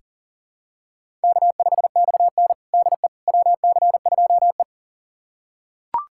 Morse code, heard as a keyed tone, about 37 wpm, 700 Hz. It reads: K5XN DE WC2E
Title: USA Callsign DE Callsign - 4 Characters - 40wpm